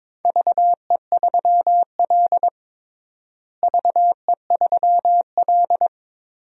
Morse code, heard as a keyed tone, about 22 wpm, 700 Hz. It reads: VE3L VE3L
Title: DXCC-100 Call Signs - 4 Characters - Repeated 3x - 22wpm